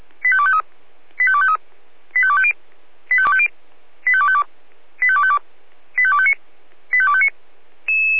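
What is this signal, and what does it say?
Five-tone selective call (ZVEI), heard as quick runs of five tones, tones of about 70 ms. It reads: 85253 85253 85250 85250 85252 85252 85259 85259
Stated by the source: ZVEI five-tone call 85810 received by SDR, recorded by feuerwehr-orlamuende.org fire brigade.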